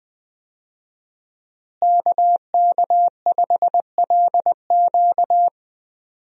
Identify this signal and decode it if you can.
Morse code, heard as a keyed tone, about 21 wpm, 700 Hz. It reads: KK5LQ
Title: USA Call Signs - 5 Characters - Repeated 2x - 20wpm